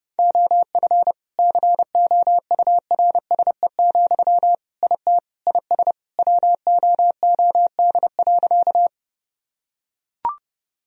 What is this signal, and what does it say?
Morse code, heard as a keyed tone, about 32 wpm, 700 Hz. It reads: OF COURSE, IT IS WOOD.
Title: Sentences from Top 1000 Words - Review Part 2 - 30wpm